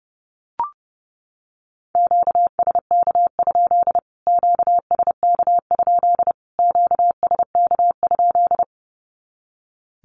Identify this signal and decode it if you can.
Morse code, heard as a keyed tone, about 30 wpm, 700 Hz. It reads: QSK? QSK? QSK?
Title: Q-Codes Repeated 3x 30wpm